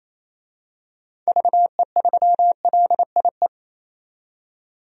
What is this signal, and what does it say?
Morse code, heard as a keyed tone, about 27 wpm, 700 Hz. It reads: VE3LIE
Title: DXCC-100 Call Signs - 6 Characters - 28wpm